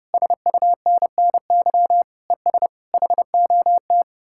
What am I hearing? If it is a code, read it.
Morse code, heard as a keyed tone, about 38 wpm, 700 Hz. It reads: SUNNY ES HOT